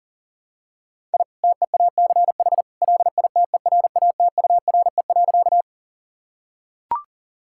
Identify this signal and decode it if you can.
Morse code, heard as a keyed tone, about 40 wpm, 700 Hz. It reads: I TEACH LITERATURE.
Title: Sentences from Top 1000 Words - Review Part 6 - 40wpm